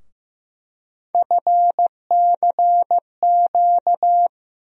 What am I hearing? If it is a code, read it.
Morse code, heard as a keyed tone, about 15 wpm, 700 Hz. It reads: FCQ
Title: USA Call Sign Postfixes - 3 Letters 15wpm